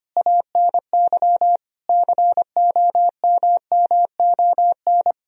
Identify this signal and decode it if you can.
Morse code, heard as a keyed tone, about 25 wpm, 690 Hz. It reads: ANY COMMON
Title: Sentences from Top 1000 Words - Review Part 4 - 25wpm